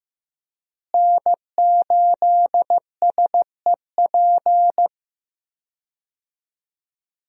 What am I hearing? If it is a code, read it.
Morse code, heard as a keyed tone, about 15 wpm, 700 Hz. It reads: N8SEP
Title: USA Call Signs - 5 Characters - Repeated 3x - 15wpm